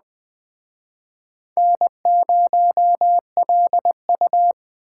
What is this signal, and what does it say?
Morse code, heard as a keyed tone, about 20 wpm, 700 Hz. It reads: N0LU
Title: Speed Racing - USA 1x2 Call Signs 17wpm